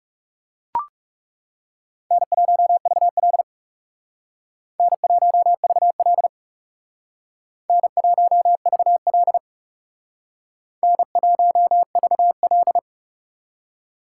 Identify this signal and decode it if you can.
Morse code, heard as a keyed tone, about 39 wpm, 700 Hz. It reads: N1VL N1VL N1VL N1VL